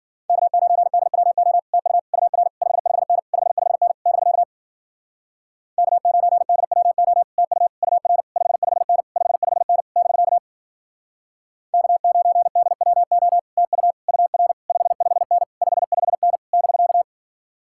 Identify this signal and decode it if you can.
Morse code, heard as a keyed tone, about 52 wpm, 700 Hz. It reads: K9DWO TU UR 55N 55N <BK> K9DWO TU UR 55N 55N <BK> K9DWO TU UR 55N 55N <BK>